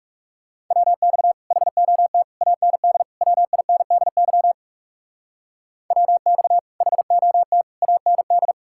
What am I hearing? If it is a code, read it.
Morse code, heard as a keyed tone, about 40 wpm, 700 Hz. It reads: WX HOT AND WINDY WX HOT AND